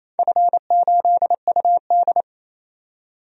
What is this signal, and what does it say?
Morse code, heard as a keyed tone, about 27 wpm, 700 Hz. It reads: F8UD
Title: DXCC-100 Call Signs - 4 Characters - 28wpm